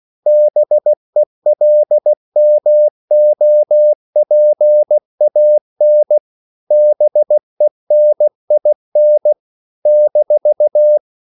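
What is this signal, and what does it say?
Morse code, heard as a keyed tone, about 16 wpm, 600 Hz. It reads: BELMOPAN BENIN -
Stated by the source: CW practice of national capitals at 16 words per minute, from ON6ZQ.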